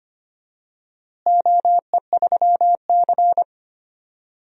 Morse code, heard as a keyed tone, about 25 wpm, 700 Hz. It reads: OE3C
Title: DXCC-100 Call Signs - 4 Characters - Repeated 3x - 25wpm